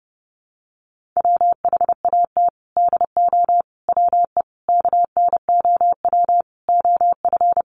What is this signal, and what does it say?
Morse code, heard as a keyed tone, about 30 wpm, 700 Hz. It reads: WHAT DO WE KNOW OF
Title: Sentences from Top 1400 Words 30wpm